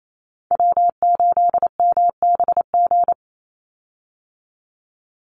Morse code, heard as a keyed tone, about 28 wpm, 700 Hz. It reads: W8MBG
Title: USA Call Signs 28wpm